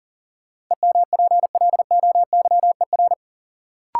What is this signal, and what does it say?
Morse code, heard as a keyed tone, about 38 wpm, 700 Hz. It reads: EMPLOYER
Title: Sentences from Top 1600 Words - New Word Drill Alt - 40wpm